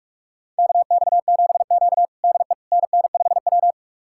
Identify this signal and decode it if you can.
Morse code, heard as a keyed tone, about 42 wpm, 700 Hz. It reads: KX7Q DE NN5W
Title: USA Callsign DE Callsign - 4 Characters - 45wpm